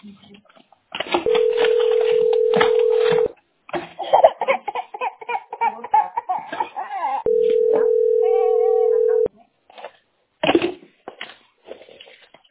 Ringback tone (North America), 440 + 480 Hz.
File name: ae4647d2-6ba8-4f46-aa4c